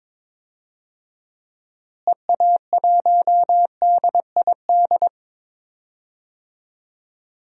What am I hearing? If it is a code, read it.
Morse code, heard as a keyed tone, about 22 wpm, 700 Hz. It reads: EA1DID
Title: DXCC-100 Call Signs - 6 Characters - Repeated 2x - 22wpm